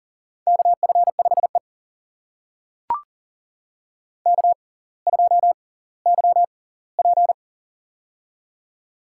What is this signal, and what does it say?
Morse code, heard as a keyed tone, about 40 wpm, spaced as for 20 wpm, 700 Hz. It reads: KF5E K2YP
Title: USA Call Signs - 4 Characters - 6x Character Spacing - 40wpm